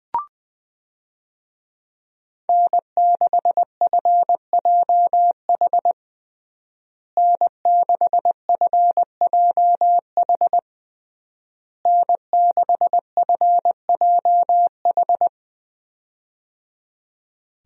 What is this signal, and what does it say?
Morse code, heard as a keyed tone, about 20 wpm, 700 Hz. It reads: N6FJH N6FJH N6FJH
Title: USA Call Signs - 5 Characters - Repeated 3x - 20wpm